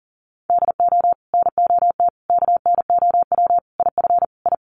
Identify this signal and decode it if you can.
Morse code, heard as a keyed tone, about 40 wpm, 700 Hz. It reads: DO NOT KNOW IF I
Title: Sets of 5 Words 40wpm